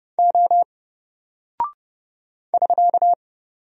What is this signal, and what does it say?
Morse code, heard as a keyed tone, about 30 wpm, 700 Hz. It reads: O <SK>
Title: Single Character (Letter:Number:Punctuation:Prosign) Alt 30wpm